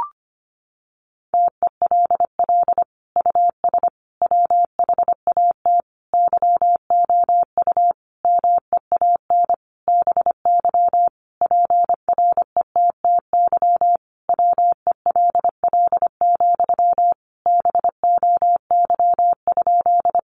Morse code, heard as a keyed tone, about 25 wpm, 700 Hz. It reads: TELL US WHAT YOU MEAN BY PRETTY WELL, BOY?
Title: Sentences from Top 1000 Words 25wpm